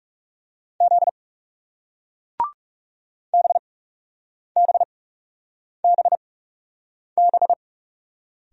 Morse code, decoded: Z BBBB

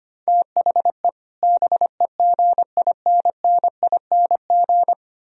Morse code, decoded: THE BEGINNING

